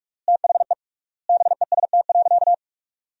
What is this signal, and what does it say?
Morse code, heard as a keyed tone, about 46 wpm, 700 Hz. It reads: THE BEST.
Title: 3x Word-Spacing - Sentences from Top 500 Words - 45wpm